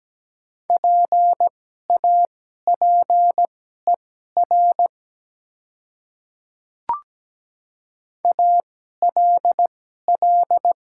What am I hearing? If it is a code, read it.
Morse code, heard as a keyed tone, about 17 wpm, spaced as for 12 wpm, 700 Hz. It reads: PAPER ALL
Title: 2x Character Spacing - Top 300 Words - 17wpm